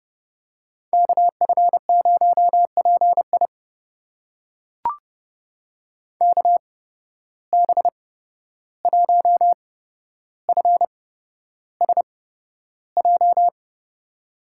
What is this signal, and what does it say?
Morse code, heard as a keyed tone, about 30 wpm, spaced as for 12 wpm, 700 Hz. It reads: KF0PI KB1FSJ
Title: USA Call Signs - 8x Character Spacing - 30wpm